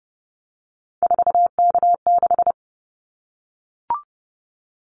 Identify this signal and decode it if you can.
Morse code, heard as a keyed tone, about 30 wpm, 700 Hz. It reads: 4K6
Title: Callsign Prefixes for Top 100 DXCC - 30wpm